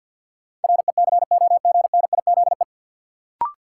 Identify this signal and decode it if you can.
Morse code, heard as a keyed tone, about 48 wpm, 700 Hz. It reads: RECOGNIZE
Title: Sentences from Top 1400 Words - New Word Drill Alt - 50wpm